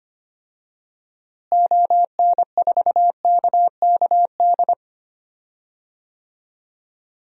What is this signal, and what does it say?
Morse code, heard as a keyed tone, about 25 wpm, 700 Hz. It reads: ON4KKD
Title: DXCC-100 Call Signs - 6 Characters - Repeated 2x - 25wpm